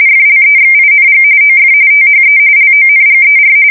RTTY, about 45.45 baud. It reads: FLEXING OF THE LONGER
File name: fsk170 / teletype-corp